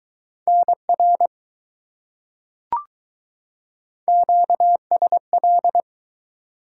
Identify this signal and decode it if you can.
Morse code, heard as a keyed tone, about 20 wpm, 700 Hz. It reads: NR QSL